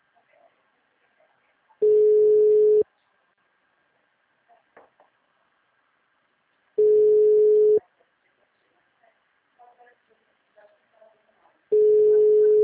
Ringback tone (Europe, 425 Hz).
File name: out- -2017